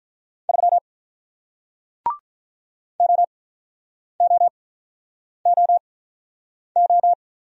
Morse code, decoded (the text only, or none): 3 OOOO